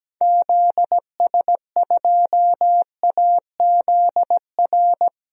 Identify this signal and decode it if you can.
Morse code, heard as a keyed tone, about 17 wpm, 700 Hz. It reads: ZS2AZR
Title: DXCC-100 Call Signs - 6 Characters - 17wpm